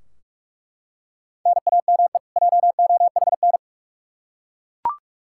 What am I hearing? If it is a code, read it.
Morse code, heard as a keyed tone, about 42 wpm, 700 Hz. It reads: NAME JOHN
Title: Practice QSO - Exchange 1 - Beginning - 45wpm